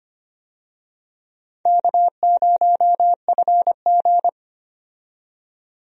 Morse code, heard as a keyed tone, about 25 wpm, 700 Hz. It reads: K0FG
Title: USA Call Signs - 4 Characters - 25wpm